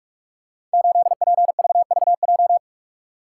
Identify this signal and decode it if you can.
Morse code, heard as a keyed tone, about 44 wpm, 700 Hz. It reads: 8PVVJ
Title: Code Groups - 5 Letter:Numbers - 45wpm